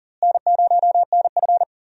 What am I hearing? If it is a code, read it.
Morse code, heard as a keyed tone, about 40 wpm, 700 Hz. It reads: N0NF
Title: USA Call Signs - 4 Characters - Repeated 3x - 40wpm